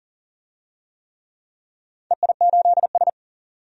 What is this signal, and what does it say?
Morse code, heard as a keyed tone, about 43 wpm, 700 Hz. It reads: EI8S
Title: DXCC-100 Call Signs - 4 Characters - 40wpm